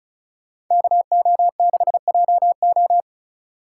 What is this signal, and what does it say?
Morse code, heard as a keyed tone, about 39 wpm, 700 Hz. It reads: KO6JO